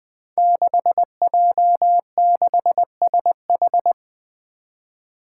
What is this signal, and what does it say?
Morse code, heard as a keyed tone, about 20 wpm, 700 Hz. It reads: 6J6SH
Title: Five Challenging Letter-Number Combinations 20wpm